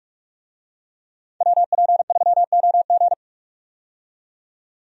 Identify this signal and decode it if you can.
Morse code, heard as a keyed tone, about 47 wpm, 700 Hz. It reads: WP3OG
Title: USA Call Signs - 5 Characters - 45wpm